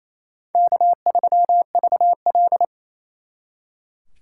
Morse code, heard as a keyed tone, about 36 wpm, 700 Hz. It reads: K3VL